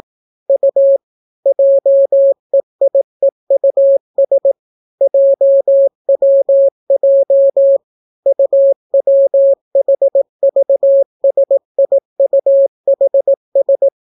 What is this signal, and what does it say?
Morse code, heard as a keyed tone, about 17 wpm, 550 Hz. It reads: U JEIEUS JWJ UWHVSIUHS